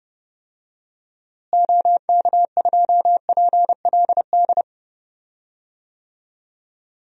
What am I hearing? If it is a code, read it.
Morse code, heard as a keyed tone, about 30 wpm, 700 Hz. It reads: OK2PLD
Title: DXCC-100 Call Signs - 6 Characters - Repeated 3x - 30wpm